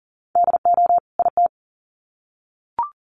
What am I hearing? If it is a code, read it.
Morse code, heard as a keyed tone, about 40 wpm, 700 Hz. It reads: DO IT
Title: Sets of 2 Words 40wpm